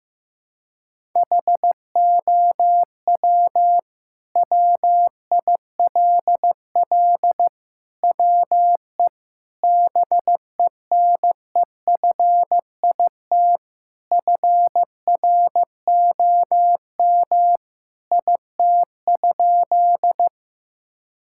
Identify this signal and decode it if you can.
Morse code, heard as a keyed tone, about 16 wpm, 700 Hz. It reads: HOW WILL WE BENEFIT FROM IT?
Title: Sentences from Top 1000 Words - Review Part 6 - 15wpm